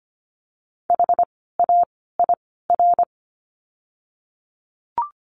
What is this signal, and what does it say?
Morse code, heard as a keyed tone, about 25 wpm, 700 Hz. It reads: H A I R
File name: course / 25:10wpm / 094 Lesson 16f - Letter c - Words 25wpm